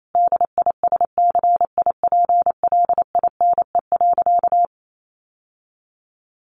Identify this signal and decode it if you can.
Morse code, heard as a keyed tone, about 28 wpm, 700 Hz. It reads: DISCIPLINE.
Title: Sentences from Top 900 Words 28wpm